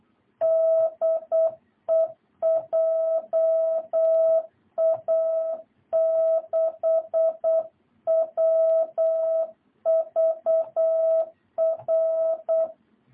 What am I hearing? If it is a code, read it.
Morse code, heard as a keyed tone, about 8 wpm, 660 Hz. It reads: DE JA6WVR
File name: jh8cbh / 2022cw